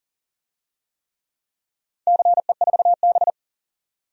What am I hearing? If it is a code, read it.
Morse code, heard as a keyed tone, about 40 wpm, 700 Hz. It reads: CE4B
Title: DXCC-100 Call Signs - 4 Characters - 40wpm